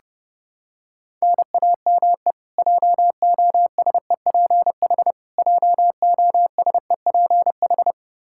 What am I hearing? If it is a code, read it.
Morse code, heard as a keyed tone, about 32 wpm, 700 Hz. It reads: NAME JOSEPH JOSEPH
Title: Practice QSO - Exchange 1 - Beginning - 30wpm